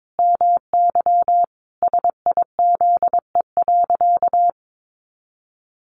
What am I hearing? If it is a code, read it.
Morse code, heard as a keyed tone, about 22 wpm, 700 Hz. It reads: MY SIZE.
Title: Sentences from Top 400 Words 22wpm